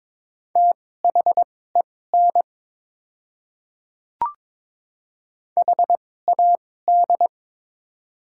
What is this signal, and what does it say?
Morse code, heard as a keyed tone, about 22 wpm, spaced as for 16 wpm, 700 Hz. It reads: THEN HAD
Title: 2x Character Spacing - Top 100 Words - 22wpm